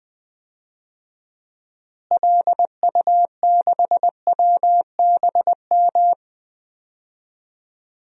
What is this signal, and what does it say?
Morse code, heard as a keyed tone, about 20 wpm, 700 Hz. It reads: LU6WBM
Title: DXCC-100 Call Signs - 6 Characters - Repeated 2x - 20wpm